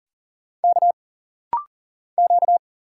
Morse code, heard as a keyed tone, about 41 wpm, 700 Hz. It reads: K Q
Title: Single Letter-Number - Rapid-Fire 40wpm